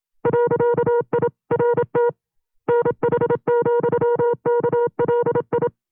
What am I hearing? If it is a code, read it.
Morse code, heard as a keyed tone, about 27 wpm, 480 Hz. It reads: .IRT NH,KLI
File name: FreeFiles / CW-practice / 25WPM-side2